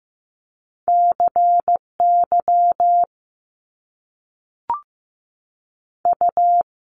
Morse code, heard as a keyed tone, about 15 wpm, 700 Hz. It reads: C Y U
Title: Top Two Letter Combinations in Words 15wpm